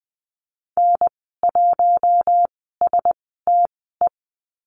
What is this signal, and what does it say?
Morse code, heard as a keyed tone, about 20 wpm, spaced as for 14 wpm, 700 Hz. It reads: N1STE
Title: 055 Lesson 10g - Letter r - Call Signs 20wpm